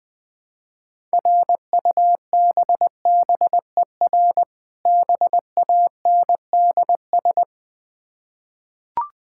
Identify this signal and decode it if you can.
Morse code, heard as a keyed tone, about 20 wpm, 700 Hz. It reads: RUBBER BANDS
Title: Difficult Words with Hints - 20wpm